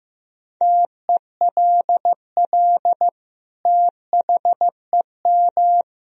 Morse code, heard as a keyed tone, about 15 wpm, 700 Hz. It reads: TELL THEM
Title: Sentences from Top 1000 Words - Review Part 2 - 15wpm